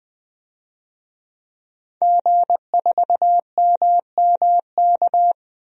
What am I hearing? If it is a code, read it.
Morse code, heard as a keyed tone, about 20 wpm, 700 Hz. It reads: G4MMK
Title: DXCC-100 Call Signs - 5 Characters - Repeated 3x - 20wpm